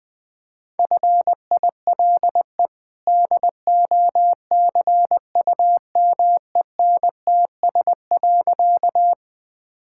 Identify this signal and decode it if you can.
Morse code, heard as a keyed tone, about 20 wpm, 700 Hz. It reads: FILE DOCUMENTS.
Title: Easy Sentences with Hints - 20wpm